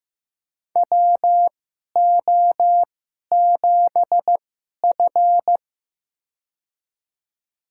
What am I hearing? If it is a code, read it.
Morse code, heard as a keyed tone, about 15 wpm, spaced as for 11 wpm, 700 Hz. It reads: WO7F
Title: USA Call Signs - 4 Characters - 2x Character Spacing - 15wpm